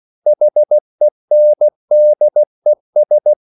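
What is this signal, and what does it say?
Morse code, heard as a keyed tone, about 16 wpm, 600 Hz. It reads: HENDES
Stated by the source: Slow (QRS) Morse code at 16 wpm, from CW med Gnister.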